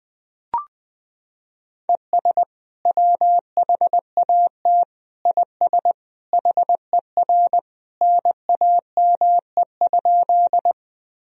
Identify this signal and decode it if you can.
Morse code, heard as a keyed tone, about 21 wpm, 700 Hz. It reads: ES WHAT IS HER NAME?
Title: Sentences from Top 200 Words and Abbreviations 20wpm